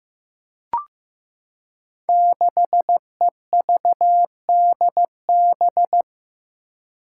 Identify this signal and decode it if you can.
Morse code, heard as a keyed tone, about 15 wpm, 700 Hz. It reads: 6EVDB
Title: Five Challenging Letter-Number Combinations 15wpm